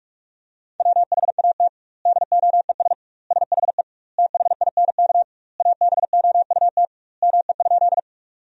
Morse code, heard as a keyed tone, about 46 wpm, 700 Hz. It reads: WHAT DOES SHE THINK ABOUT ME?
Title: 2x Word-Spacing - Sentences from Top 200 Words - 45wpm